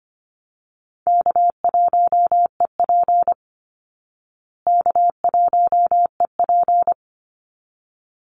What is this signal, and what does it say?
Morse code, heard as a keyed tone, about 25 wpm, 700 Hz. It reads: K1EP K1EP
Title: Call Signs Repeated 3x - 25wpm